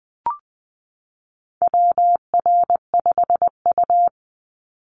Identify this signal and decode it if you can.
Morse code, heard as a keyed tone, about 19 wpm, 700 Hz. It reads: WR5U
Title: Call Signs - 20wpm